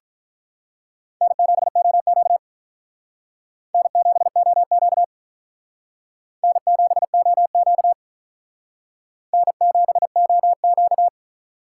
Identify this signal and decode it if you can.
Morse code, heard as a keyed tone, about 46 wpm, 700 Hz. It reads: N7OQ N7OQ N7OQ N7OQ